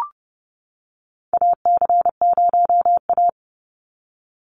Morse code, heard as a keyed tone, about 30 wpm, 700 Hz. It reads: AC0A
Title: NA SOTA Chaser Callsigns 30wpm